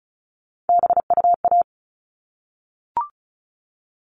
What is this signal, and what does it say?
Morse code, heard as a keyed tone, about 34 wpm, 700 Hz. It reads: BUA
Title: All Cut Numbers - 3-Digit 35wpm